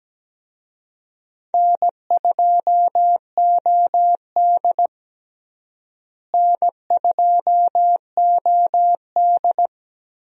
Morse code, heard as a keyed tone, about 17 wpm, 700 Hz. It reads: N2OD N2OD
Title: USA Call Signs - 4 Characters - Repeated 3x - 17wpm